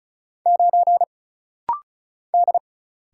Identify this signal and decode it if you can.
Morse code, heard as a keyed tone, about 36 wpm, 700 Hz. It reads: 9 D
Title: Single Letter-Number - Rapid-Fire 35wpm